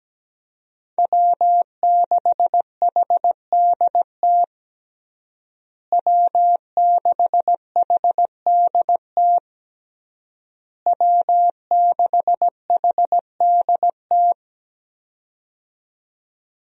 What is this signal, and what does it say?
Morse code, heard as a keyed tone, about 17 wpm, 700 Hz. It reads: W6HDT W6HDT W6HDT
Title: USA Call Signs - 5 Characters - Repeated 3x - 17wpm